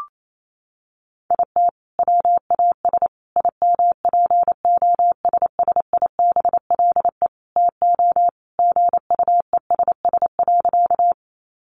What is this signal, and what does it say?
Morse code, heard as a keyed tone, about 28 wpm, 700 Hz. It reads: IT WAS IMPOSSIBLE TO GUESS.
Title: Sentences from Top 1500 Words 28wpm